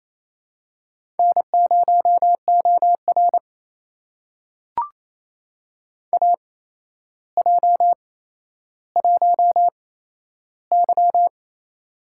Morse code, heard as a keyed tone, about 28 wpm, spaced as for 12 wpm, 700 Hz. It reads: N0OR AJ1Y
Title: USA Call Signs - 4 Characters - 8x Character Spacing - 28wpm